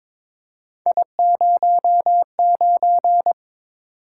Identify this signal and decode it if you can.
Morse code, heard as a keyed tone, about 23 wpm, 700 Hz. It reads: I09
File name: practice / 22wpm / Three Letter-Number Combinations - 22wpm